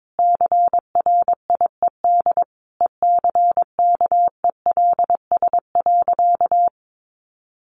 Morse code, heard as a keyed tone, about 22 wpm, 700 Hz. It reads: CRIED ECKELS.
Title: A Sound Of Thunder 22wpm